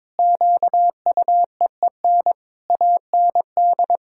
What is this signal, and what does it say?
Morse code, heard as a keyed tone, about 22 wpm, 700 Hz. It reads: QUEEN AND
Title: Pangram Challenge v2 - 22wpm